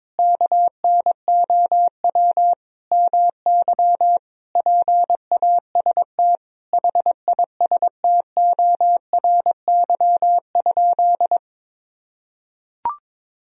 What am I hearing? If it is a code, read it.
Morse code, heard as a keyed tone, about 22 wpm, 700 Hz. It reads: KNOW MY PAST HISTORY?
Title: Sentences from Top 1000 Words - Review Part 3 - 22wpm